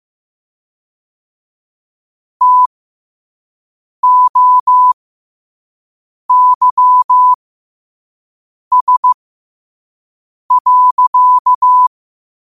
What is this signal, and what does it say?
Morse code, heard as a keyed tone, about 15 wpm, spaced as for 5 wpm, 1000 Hz. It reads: TOYS.
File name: qotdfiles / 5wpm